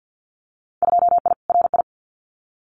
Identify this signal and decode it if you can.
Morse code, heard as a keyed tone, about 50 wpm, 700 Hz. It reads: 2I RI
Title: Winter Field Day Exchanges Encore 50wpm